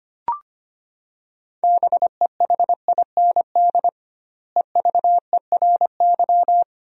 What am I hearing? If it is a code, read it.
Morse code, heard as a keyed tone, about 25 wpm, 700 Hz. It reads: BEHIND EVERY
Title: 2x Word-Spacing - Sentences from Top 500 Words - 25wpm